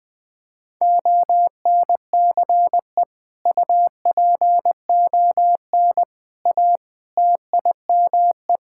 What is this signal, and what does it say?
Morse code, heard as a keyed tone, about 20 wpm, 700 Hz. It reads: ONCE UPON A TIME
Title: Sentences with 7-Word Phrases - 20wpm